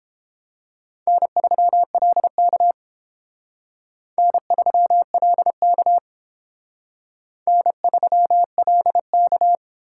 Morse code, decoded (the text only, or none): N3LK N3LK N3LK